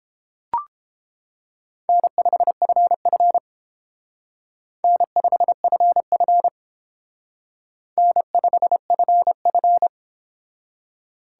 Morse code, heard as a keyed tone, about 28 wpm, 700 Hz. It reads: N5FF N5FF N5FF